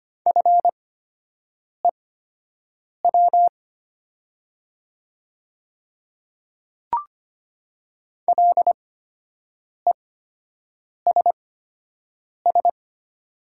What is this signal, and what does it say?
Morse code, heard as a keyed tone, about 25 wpm, spaced as for 7 wpm, 700 Hz. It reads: FEW LESS
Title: 8x Character Spacing - Top 500 Words - 25wpm